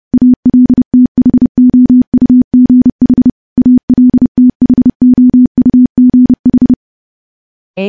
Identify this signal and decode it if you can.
Morse code, heard as a keyed tone, about 30 wpm, 250 Hz. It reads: ALTHOUGH ALTHOUGH
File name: morse / morse-training-30WPM